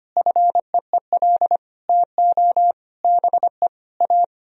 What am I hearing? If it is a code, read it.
Morse code, heard as a keyed tone, about 25 wpm, 700 Hz. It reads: FEEL TO BE A
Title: Sentences from Top 1000 Words - Review Part 2 - 25wpm